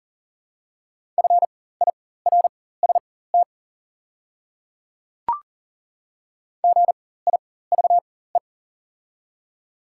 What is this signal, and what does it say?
Morse code, heard as a keyed tone, about 40 wpm, spaced as for 19 wpm, 690 Hz. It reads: FIRST GIVE
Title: 4x Character Spacing - Top 200 Words - 40wpm